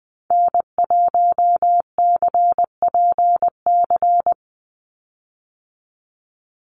Morse code, heard as a keyed tone, about 20 wpm, 700 Hz. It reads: N1CPC
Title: USA Call Signs 20wpm